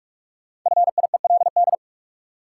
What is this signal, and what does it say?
Morse code, heard as a keyed tone, about 46 wpm, 700 Hz. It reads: FIELD
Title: Sentences from Top 1000 Words - New Word Drill - 45wpm